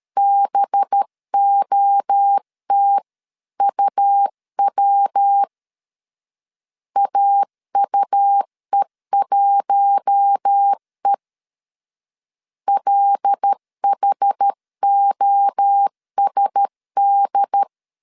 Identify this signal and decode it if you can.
Morse code, heard as a keyed tone, about 13 wpm, 800 Hz. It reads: BOTUW AUE1E LHOSD
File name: CW / DET / 1 6 1500C S C 12wpm 3533